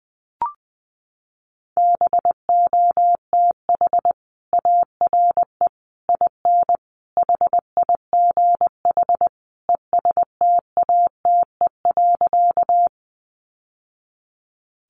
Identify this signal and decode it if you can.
Morse code, heard as a keyed tone, about 20 wpm, 700 Hz. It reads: BOTH ARE IN HIGH ESTATE.
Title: Sentences from Top 1200 Words 20wpm